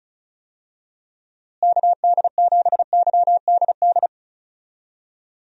Morse code, heard as a keyed tone, about 34 wpm, 700 Hz. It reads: KD7YDD
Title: USA Call Signs - 6 Characters - Repeated 2x - 35wpm